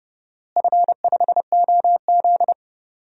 Morse code, heard as a keyed tone, about 30 wpm, 700 Hz. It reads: F5OZ
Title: DXCC-100 Call Signs - 4 Characters - Repeated 2x - 30wpm